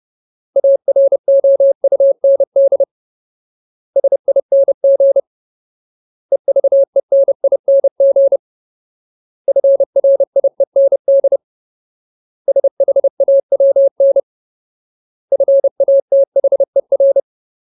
Morse code, heard as a keyed tone, about 30 wpm, 550 Hz. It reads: AROUND SING EVENING FRIEND SHAWN FATHER
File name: Ultimate Word List / 30WPM Ultimate